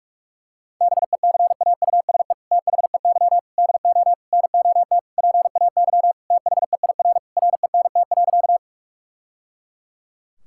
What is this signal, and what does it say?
Morse code, heard as a keyed tone, about 46 wpm, 700 Hz. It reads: BECAUSE THEY DO NOT PAY THEIR RENT.
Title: Sentences from Top 1000 Words - Review Part 5 - 45wpm